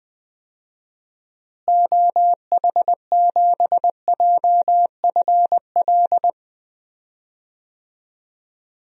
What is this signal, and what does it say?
Morse code, heard as a keyed tone, about 20 wpm, 700 Hz. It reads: OH7JFL
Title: DXCC-100 Call Signs - 6 Characters - Repeated 3x - 20wpm